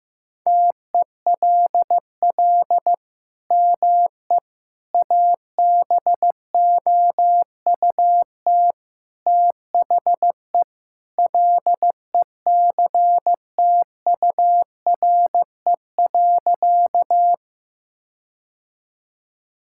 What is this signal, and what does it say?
Morse code, heard as a keyed tone, about 15 wpm, 700 Hz. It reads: TELL ME ABOUT THE LECTURE.
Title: Sentences from Top 1600 Words 15wpm